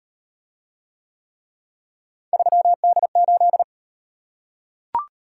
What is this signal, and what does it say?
Morse code, heard as a keyed tone, about 34 wpm, 700 Hz. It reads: 3D8